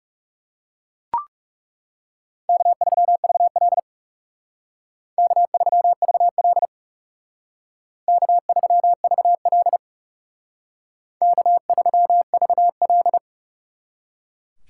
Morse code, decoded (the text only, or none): K3VL K3VL K3VL K3VL